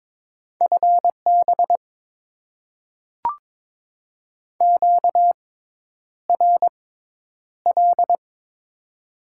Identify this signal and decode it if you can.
Morse code, heard as a keyed tone, about 22 wpm, spaced as for 10 wpm, 700 Hz. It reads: FB QRL
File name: Top 20 QSO Elements - 6x Character Spacing - 22wpm